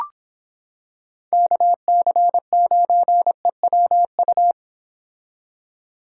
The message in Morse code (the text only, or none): KC9EWU